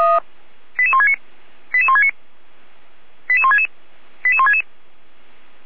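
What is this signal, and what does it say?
Five-tone selective call (ZVEI), heard as quick runs of five tones, tones of about 70 ms. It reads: 88169 88169 88166 88166
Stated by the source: ZVEI five-tone selective call received by SDR.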